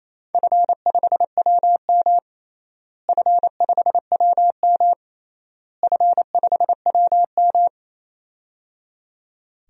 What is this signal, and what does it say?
Morse code, heard as a keyed tone, about 28 wpm, 700 Hz. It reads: F5WM F5WM F5WM
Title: DXCC-100 Call Signs - 4 Characters - Repeated 3x - 28wpm